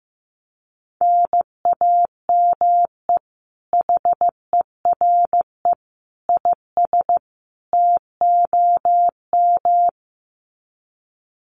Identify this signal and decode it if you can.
Morse code, heard as a keyed tone, about 15 wpm, 700 Hz. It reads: NAME HERE IS TOM
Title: Common QSO Phrases 15wpm